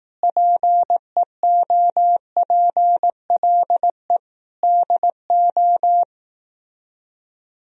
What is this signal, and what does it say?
Morse code, heard as a keyed tone, about 18 wpm, 700 Hz. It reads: PEOPLE DO
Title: Speed-Racing - Sets of 5 Words 15wpm